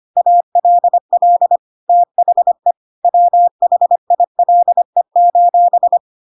Morse code, heard as a keyed tone, about 25 wpm, 700 Hz. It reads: ALL THE WHILE:
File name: Pinocchio / Pinocchio ch5 25 25 ik1hge